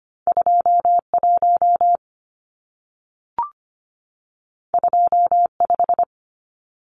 Morse code, heard as a keyed tone, about 25 wpm, 700 Hz. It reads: 21 25
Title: Numbers - 2-Digit 25wpm